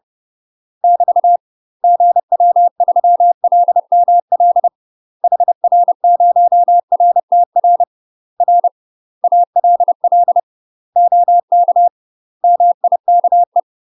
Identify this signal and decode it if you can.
Morse code, heard as a keyed tone, about 30 wpm, 700 Hz. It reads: <BT> GW3LML HR0RTR R ALL OK MIKE
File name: UK MASTER QSOs TRAINING / 30WPM / UK MASTER QSOs 0007